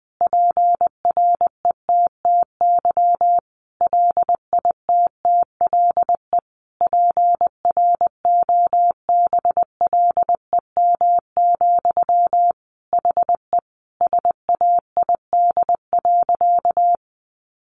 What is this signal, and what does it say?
Morse code, heard as a keyed tone, about 20 wpm, 700 Hz. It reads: PRETTY LITTLE PROBLEM, HE SAID.
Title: Sentences from Top 1000 Words 20wpm